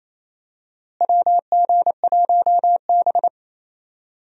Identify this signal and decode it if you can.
Morse code, heard as a keyed tone, about 28 wpm, 700 Hz. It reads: WG1B
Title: USA Call Signs - 4 Characters - Repeated 2x - 28wpm